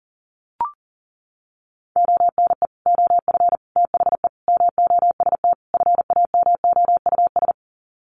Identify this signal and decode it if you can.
Morse code, heard as a keyed tone, about 40 wpm, 700 Hz. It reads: ONE OF THE MOST FAMOUS
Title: Sets of 5 Words Encore2 40wpm